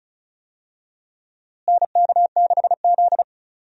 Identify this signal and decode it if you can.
Morse code, heard as a keyed tone, about 36 wpm, 700 Hz. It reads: NK6Z
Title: USA Call Signs - 4 Characters - Repeated 3x - 35wpm